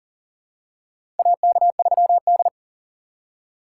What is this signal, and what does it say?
Morse code, heard as a keyed tone, about 40 wpm, 700 Hz. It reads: AK3D
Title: USA Call Signs - 4 Characters - 3x Character Spacing - 40wpm